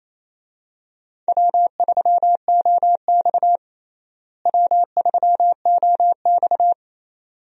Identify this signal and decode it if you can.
Morse code, heard as a keyed tone, about 28 wpm, 700 Hz. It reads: W3OX W3OX
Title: USA Call Signs - 4 Characters - Repeated 2x - 28wpm